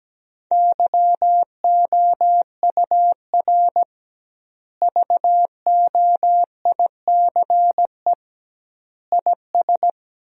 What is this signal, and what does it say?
Morse code, heard as a keyed tone, about 17 wpm, 700 Hz. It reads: YOUR VOICE IS
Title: 2x Word-Spacing - Sentences from Top 500 Words - 17wpm